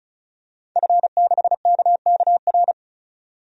Morse code, heard as a keyed tone, about 36 wpm, 700 Hz. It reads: F6KKR
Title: DXCC-100 Call Signs - 5 Characters - 35wpm